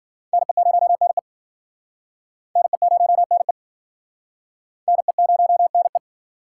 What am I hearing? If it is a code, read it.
Morse code, heard as a keyed tone, about 54 wpm, 700 Hz. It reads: NE0NE NE0NE NE0NE